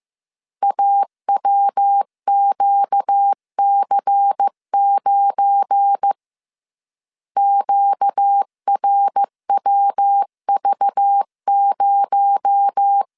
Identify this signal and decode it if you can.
Morse code, heard as a keyed tone, about 15 wpm, 800 Hz. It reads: AWQC9 QRWV0
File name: CW / DET / 1 9 1500C S A 14wpm 3222